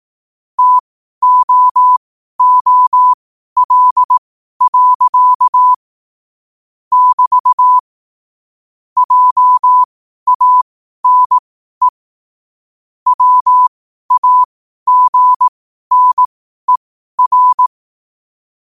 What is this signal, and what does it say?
Morse code, heard as a keyed tone, about 18 wpm, spaced as for 12 wpm, 1000 Hz. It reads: TOOL. <BT> JANE WAGNER
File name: qotdfiles / 12wpm